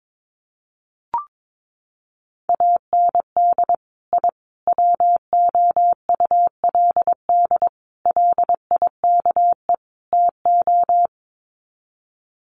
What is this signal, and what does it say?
Morse code, heard as a keyed tone, about 22 wpm, 700 Hz. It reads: AND I WOULD LIKE TO
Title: Sets of 5 Words Encore 22wpm